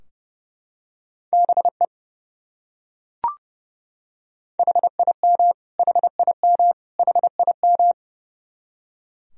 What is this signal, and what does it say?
Morse code, heard as a keyed tone, about 30 wpm, 700 Hz. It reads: BE HIM HIM HIM
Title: Top 100 Words Repeated 30wpm